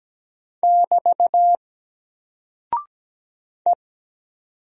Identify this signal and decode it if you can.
Morse code, heard as a keyed tone, about 17 wpm, 700 Hz. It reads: <BT> E
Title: Single Character (Letter:Number:Punctuation:Prosign) 17wpm